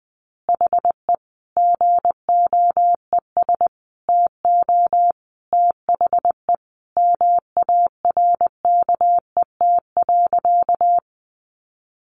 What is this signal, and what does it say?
Morse code, heard as a keyed tone, about 20 wpm, 700 Hz. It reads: HE GOES TO THE MARKET.
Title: Additional Easy Phrases 20wpm